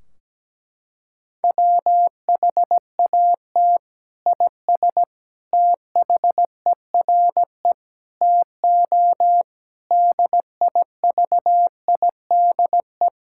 Morse code, heard as a keyed tone, about 17 wpm, 700 Hz. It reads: WHAT IS THERE TO DIVIDE
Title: Sentences from Top 1600 Words 17wpm